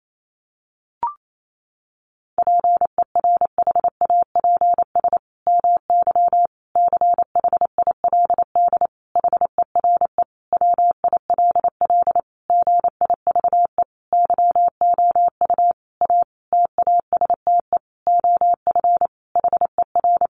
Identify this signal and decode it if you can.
Morse code, heard as a keyed tone, about 28 wpm, 700 Hz. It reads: PERHAPS MY CHILD HERE WILL GIVE YOU A TASTE OF HER
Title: Sentences from Top 1100 Words 28wpm